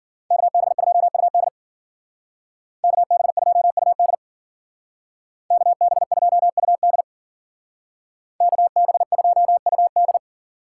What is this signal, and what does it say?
Morse code, heard as a keyed tone, about 49 wpm, 700 Hz. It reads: KB2UD KB2UD KB2UD KB2UD
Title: Speed Racing - USA Call Signs 40wpm